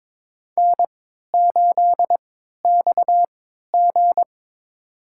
Morse code, heard as a keyed tone, about 21 wpm, spaced as for 12 wpm, 700 Hz. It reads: N8XG
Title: USA Call Signs - 4 Characters - 3x Character Spacing - 22wpm